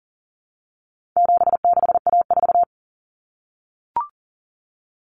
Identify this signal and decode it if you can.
Morse code, heard as a keyed tone, about 40 wpm, 700 Hz. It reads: 76A4
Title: Common Cut Numbers - 4-Digit 40wpm